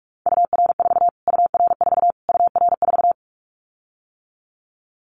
Morse code, heard as a keyed tone, about 44 wpm, 700 Hz. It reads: UR4 UR4 UR4
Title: Callsign Prefixes for Top 25 DXCC - Repeated 3x - 45wpm